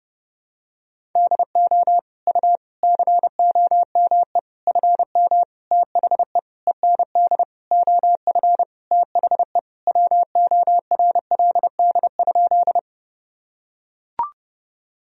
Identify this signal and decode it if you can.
Morse code, heard as a keyed tone, about 32 wpm, 700 Hz. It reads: DO U COME FM THE END OF THE WORLD?
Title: Sentences from Top 200 Words and Abbreviations 30wpm